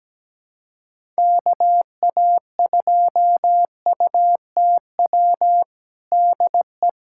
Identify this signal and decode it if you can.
Morse code, heard as a keyed tone, about 17 wpm, 700 Hz. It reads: KA2UTW DE
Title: USA Callsign DE Callsign 17wpm